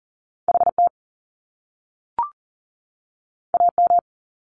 Morse code, heard as a keyed tone, about 40 wpm, 700 Hz. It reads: HT AM
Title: Top Two Letter Combinations in Words 40wpm